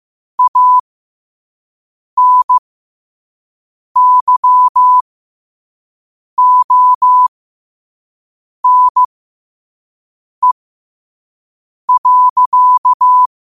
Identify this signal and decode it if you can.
Morse code, heard as a keyed tone, about 15 wpm, spaced as for 5 wpm, 1000 Hz. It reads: ANYONE.
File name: qotdfiles / 5wpm